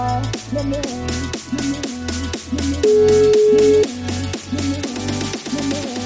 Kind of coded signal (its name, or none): telephone call progress tone